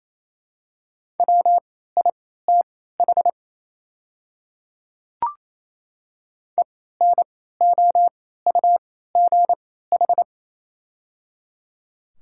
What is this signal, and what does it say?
Morse code, heard as a keyed tone, about 28 wpm, spaced as for 16 wpm, 690 Hz. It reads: WITH ENOUGH
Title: 3x Character Spacing - Top 500 Words - 28wpm